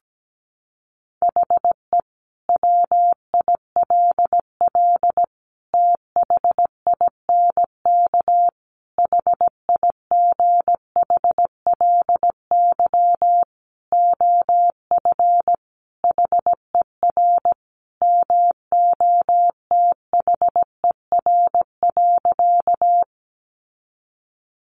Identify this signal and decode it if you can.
Morse code, heard as a keyed tone, about 17 wpm, 700 Hz. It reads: HE WILL THINK HIGHLY OF HER MOTHER.
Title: Sentences from Top 1100 Words 17wpm